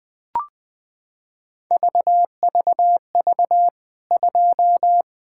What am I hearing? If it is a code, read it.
Morse code, heard as a keyed tone, about 20 wpm, 700 Hz. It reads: VVV 2
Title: K1USN SST QSO Practice 20wpm